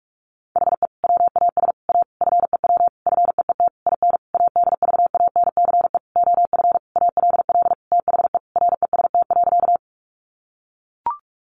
Morse code, heard as a keyed tone, about 44 wpm, 700 Hz. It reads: HE WAS A FEW FEET IN ADVANCE OF ALL THE REST.
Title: Sentences from Top 1400 Words 45wpm